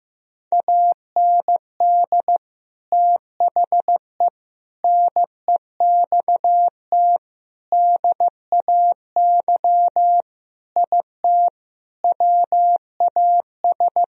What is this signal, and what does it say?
Morse code, heard as a keyed tone, about 15 wpm, 700 Hz. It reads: AND THE NEXT DAY IT WAS
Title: Sentences from Top 1000 Words - Review Part 1 - 15wpm